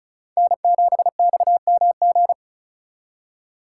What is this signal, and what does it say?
Morse code, heard as a keyed tone, about 42 wpm, 700 Hz. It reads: N7XMG
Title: Speed Racing - USA 1x3 Call Signs 35wpm